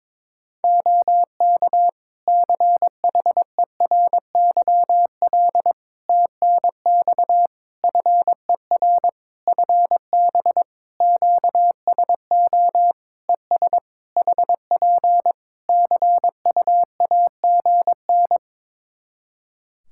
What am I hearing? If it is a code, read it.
Morse code, heard as a keyed tone, about 22 wpm, 700 Hz. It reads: OK CHERYL TNX FER FB QSO ES HP CUAGN
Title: Practice QSO - Exchange 4 - Ending - 22wpm